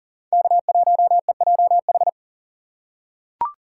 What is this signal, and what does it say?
Morse code, heard as a keyed tone, about 40 wpm, 700 Hz. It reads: K1EJH
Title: USA Call Signs - 5 Characters - 3x Character Spacing - 40wpm